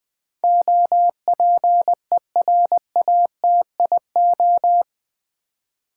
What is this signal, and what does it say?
Morse code, heard as a keyed tone, about 20 wpm, 700 Hz. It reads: OPERATIO
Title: Build A Word - 4 Letters At A Time - 11-Letter Words - 20wpm